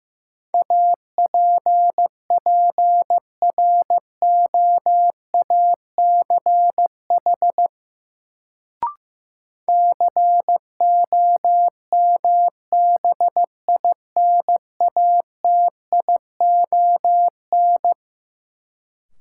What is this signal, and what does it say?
Morse code, heard as a keyed tone, about 15 wpm, 700 Hz. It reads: APPROACH COMBINATION
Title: Sentences from Top 1000 Words - Review Part 6 - New Word Drill - 15wpm